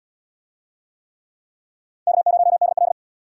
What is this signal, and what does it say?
Morse code, heard as a keyed tone, about 66 wpm, 700 Hz. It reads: K0DL